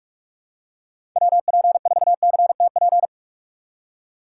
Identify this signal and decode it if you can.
Morse code, heard as a keyed tone, about 46 wpm, 700 Hz. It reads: WP4CTP
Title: USA Call Signs - 3x Character Spacing - 45wpm